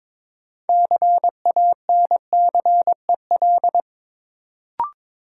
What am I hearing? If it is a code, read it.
Morse code, heard as a keyed tone, about 22 wpm, 700 Hz. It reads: CANCEL
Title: Sentences from Top 1200 Words - New Word Drill - 22wpm